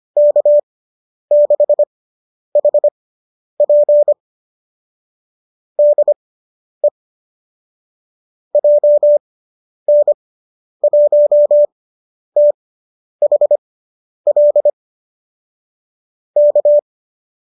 Morse code, heard as a keyed tone, about 25 wpm, spaced as for 10 wpm, 590 Hz. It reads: K6HP DE JN1THL K